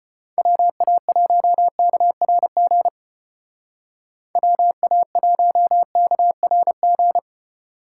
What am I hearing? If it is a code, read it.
Morse code, heard as a keyed tone, about 32 wpm, 700 Hz. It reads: WA1KRG WA1KRG